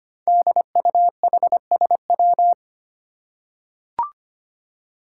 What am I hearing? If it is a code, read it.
Morse code, heard as a keyed tone, about 25 wpm, 700 Hz. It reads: DUHSW
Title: Five Challenging Letter-Number Combinations 25wpm